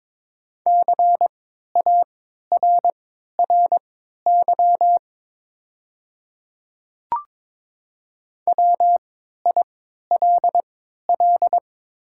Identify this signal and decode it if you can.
Morse code, heard as a keyed tone, about 22 wpm, spaced as for 12 wpm, 700 Hz. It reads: CARRY WILL
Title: 3x Character Spacing - Top 300 Words - 22wpm